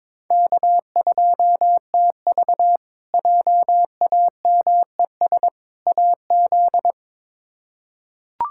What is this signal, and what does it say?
Morse code, heard as a keyed tone, about 22 wpm, 700 Hz. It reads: K2TV JAMES AZ
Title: K1USN SST QSO Practice - 22wpm